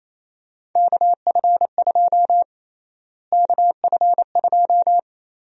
Morse code, heard as a keyed tone, about 28 wpm, 700 Hz. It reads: KF2 KF2
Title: USA Call Sign Prefixes - Repeated 2x - 28wpm